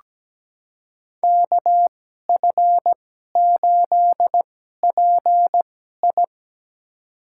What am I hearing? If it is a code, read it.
Morse code, heard as a keyed tone, about 17 wpm, spaced as for 12 wpm, 700 Hz. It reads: KF8PI
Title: USA Call Signs - 5 Characters - 2x Character Spacing - 17wpm